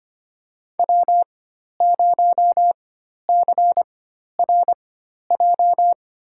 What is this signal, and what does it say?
Morse code, heard as a keyed tone, about 25 wpm, spaced as for 12 wpm, 700 Hz. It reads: W0CRJ
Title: USA Call Signs - 5 Characters - 4x Character Spacing - 25wpm